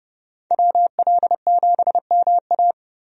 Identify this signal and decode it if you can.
Morse code, heard as a keyed tone, about 30 wpm, 700 Hz. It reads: WL7MA
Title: USA Call Signs - 5 Characters - 30wpm